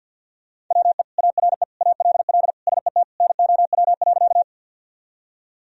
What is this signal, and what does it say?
Morse code, heard as a keyed tone, about 51 wpm, 700 Hz. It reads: WE ARE ALL SET NOW.
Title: Sentences from Top 1000 Words - Review Part 1 - 50wpm